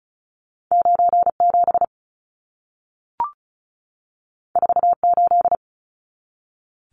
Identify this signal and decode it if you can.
Morse code, heard as a keyed tone, about 36 wpm, 700 Hz. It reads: 97 48
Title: Numbers - 2-Digit 35wpm